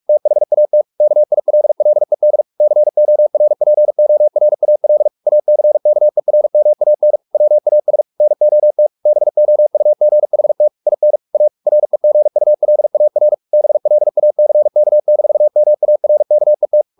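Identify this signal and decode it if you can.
Morse code, heard as a keyed tone, about 44 wpm, 600 Hz. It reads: THAT KILLED CORPORAL ACKERMAN WAS NOT BOUGHT IN A REGULAR BLACK-MARKET
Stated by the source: Morse practice at 45 wpm, the novel Never by Ken Follett in English, from CW med Gnister.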